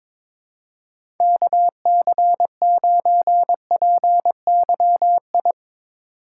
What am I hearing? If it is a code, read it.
Morse code, heard as a keyed tone, about 21 wpm, 700 Hz. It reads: KC9PYI
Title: USA Call Signs - 6 Characters - 2x Character Spacing - 22wpm